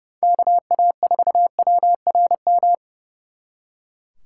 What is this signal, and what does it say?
Morse code, heard as a keyed tone, about 39 wpm, 700 Hz. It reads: KA4WRM